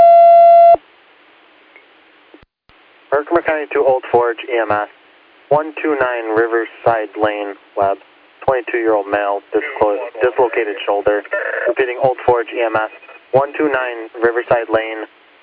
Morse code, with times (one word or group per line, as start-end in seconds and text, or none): none